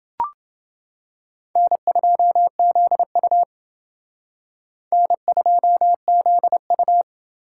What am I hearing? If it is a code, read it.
Morse code, heard as a keyed tone, about 28 wpm, 700 Hz. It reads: N2ZU N2ZU